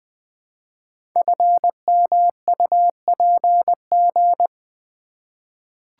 Morse code, heard as a keyed tone, about 20 wpm, 700 Hz. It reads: FMUPG
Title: Code Groups - 5 Letters - 20wpm